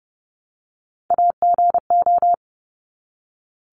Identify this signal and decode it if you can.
Morse code, heard as a keyed tone, about 30 wpm, 700 Hz. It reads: AGO
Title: Top 500 Words Repeated 30wpm